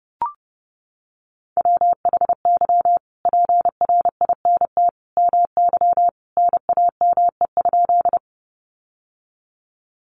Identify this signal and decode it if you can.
Morse code, heard as a keyed tone, about 30 wpm, 700 Hz. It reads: WHY PRINT MY NAME?
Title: Sentences from Top 1400 Words 30wpm